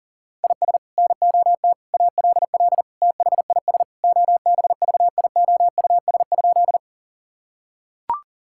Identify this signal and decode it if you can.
Morse code, heard as a keyed tone, about 40 wpm, 700 Hz. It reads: IS NOT ALL THIS OBVIOUS?
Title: Sentences from Top 1600 Words 40wpm